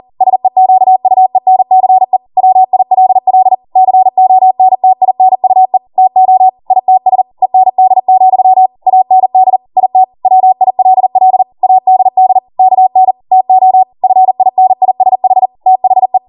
Morse code, heard as a keyed tone, about 40 wpm, 750 Hz. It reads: SEQUENCE WILL CONTINUE TO ITS END, AND IT WILL ADD KN TO FINISH THE